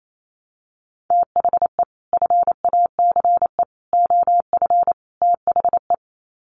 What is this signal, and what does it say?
Morse code, heard as a keyed tone, about 28 wpm, 700 Hz. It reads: THE FACE OF THE
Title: Sets of 4 Words 28wpm